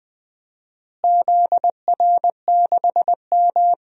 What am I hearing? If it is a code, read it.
Morse code, heard as a keyed tone, about 20 wpm, 700 Hz. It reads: ZR6M
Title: DXCC-100 Call Signs - 4 Characters - Repeated 2x - 20wpm